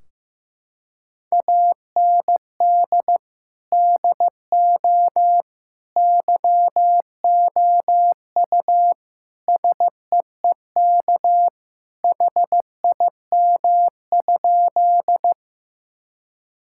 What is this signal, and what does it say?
Morse code, heard as a keyed tone, about 15 wpm, 700 Hz. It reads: AND DO YOU SEEK HIM?
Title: Sentences from Top 1600 Words 15wpm